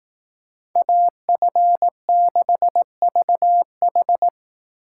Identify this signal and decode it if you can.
Morse code, heard as a keyed tone, about 18 wpm, 700 Hz. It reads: AF6VH
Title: Speed Racing - USA 2x2 Call Signs 15wpm